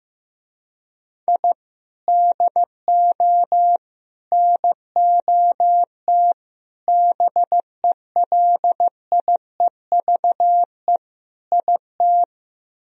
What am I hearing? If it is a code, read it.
Morse code, heard as a keyed tone, about 15 wpm, 700 Hz. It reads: I DO NOT BELIEVE IT
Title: Speed-Racing - Sets of 5 Words 15wpm